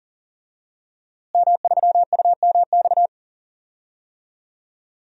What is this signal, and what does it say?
Morse code, heard as a keyed tone, about 39 wpm, 700 Hz. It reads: M3UMX
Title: DXCC-100 Call Signs - 5 Characters - Repeated 3x - 40wpm